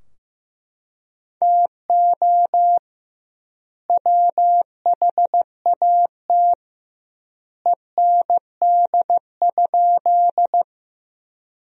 Morse code, heard as a keyed tone, about 15 wpm, 700 Hz. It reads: TO WHAT END?
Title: 2x Word-Spacing - Sentences from Top 200 Words - 15wpm